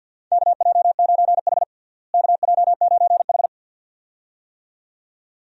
Morse code, heard as a keyed tone, about 49 wpm, 700 Hz. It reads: KJ9H KJ9H
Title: USA Call Signs - 4 Characters - Repeated 2x - 50wpm